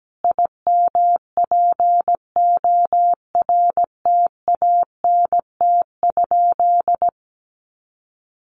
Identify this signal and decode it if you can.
Morse code, heard as a keyed tone, about 17 wpm, 700 Hz. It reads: IMPORTANT?
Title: Sentences from Top 1300 Words 17wpm